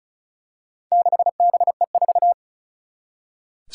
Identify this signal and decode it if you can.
Morse code, heard as a keyed tone, about 36 wpm, 700 Hz. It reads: 6BE4
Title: Four Challenging Letter-Number Combinations 35wpm